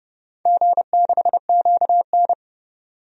Morse code, heard as a keyed tone, about 30 wpm, 700 Hz. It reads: G6QN
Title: DXCC-100 Call Signs - 4 Characters - 30wpm